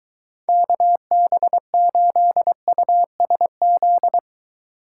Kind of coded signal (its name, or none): Morse code